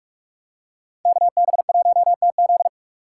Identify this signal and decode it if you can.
Morse code, heard as a keyed tone, about 46 wpm, 700 Hz. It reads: KB1TZ